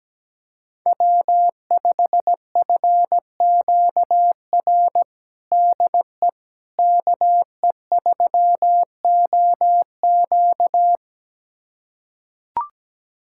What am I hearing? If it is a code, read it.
Morse code, heard as a keyed tone, about 17 wpm, 700 Hz. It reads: W5FQR DE KE3OQ
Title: USA Callsign DE Callsign - 5 Characters - 17wpm